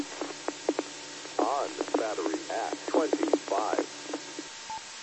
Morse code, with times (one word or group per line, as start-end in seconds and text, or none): none